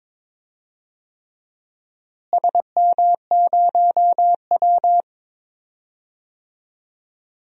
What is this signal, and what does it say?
Morse code, heard as a keyed tone, about 21 wpm, 700 Hz. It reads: SM0W
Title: DXCC-100 Call Signs - 4 Characters - Repeated 3x - 22wpm